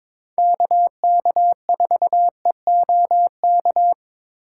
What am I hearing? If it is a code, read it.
Morse code, heard as a keyed tone, about 21 wpm, 700 Hz. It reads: KK4EOK
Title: USA Call Signs - 4x Character Spacing - 22wpm